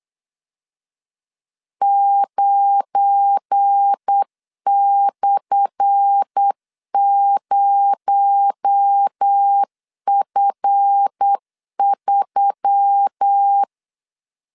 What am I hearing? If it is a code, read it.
Morse code, heard as a keyed tone, about 8 wpm, 800 Hz. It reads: 9/0F3
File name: LEZIONI / 10 SIGNS / 10D COMA 8wpm 3749